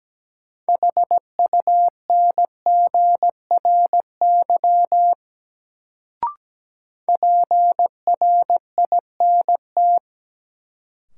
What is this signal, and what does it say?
Morse code, heard as a keyed tone, about 17 wpm, 700 Hz. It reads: HUNGRY PRINT
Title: Sentences from Top 1400 Words - New Word Drill - 17wpm